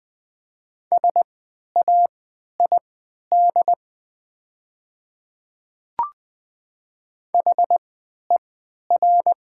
Morse code, heard as a keyed tone, about 20 wpm, spaced as for 11 wpm, 700 Hz. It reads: SAID HER
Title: 3x Character Spacing - Top 300 Words - 20wpm